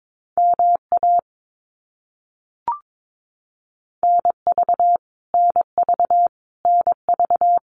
Morse code, heard as a keyed tone, about 22 wpm, 700 Hz. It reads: MA NV NV NV
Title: US States Repeated 3x 22wpm